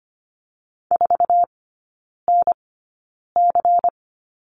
Morse code, heard as a keyed tone, about 25 wpm, spaced as for 9 wpm, 700 Hz. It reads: 4NC
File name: Lesson 17e - Letter u - Three Characters 25wpm